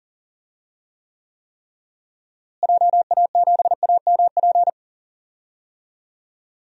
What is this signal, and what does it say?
Morse code, heard as a keyed tone, about 40 wpm, 700 Hz. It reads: JA7AMP
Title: DXCC-100 Call Signs - 6 Characters - Repeated 3x - 40wpm